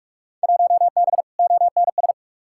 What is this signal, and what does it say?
Morse code, heard as a keyed tone, about 47 wpm, 700 Hz. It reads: 1B ONS
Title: Field Day Exchanges Encore - 45wpm